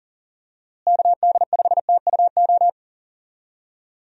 Morse code, heard as a keyed tone, about 38 wpm, 700 Hz. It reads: KD5TUO
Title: USA Call Signs - 6 Characters - Repeated 2x - 40wpm